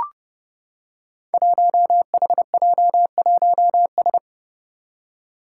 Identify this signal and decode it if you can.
Morse code, heard as a keyed tone, about 29 wpm, 700 Hz. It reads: 1HJ1S
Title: Five Challenging Letter-Number Combinations 30wpm